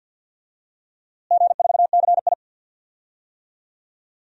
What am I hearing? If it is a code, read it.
Morse code, heard as a keyed tone, about 52 wpm, 700 Hz. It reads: G4CI
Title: DXCC-100 Call Signs - 4 Characters - Repeated 2x - 50wpm